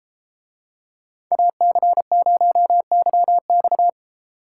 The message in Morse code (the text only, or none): AC0YX